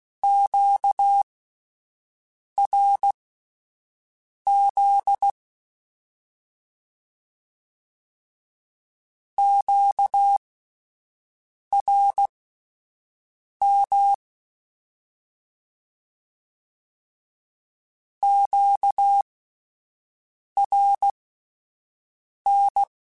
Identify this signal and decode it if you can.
Morse code, heard as a keyed tone, about 16 wpm, spaced as for 5 wpm, 800 Hz. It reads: QRZ QRM QRN
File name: Misc / Misc-mp3s / Words @5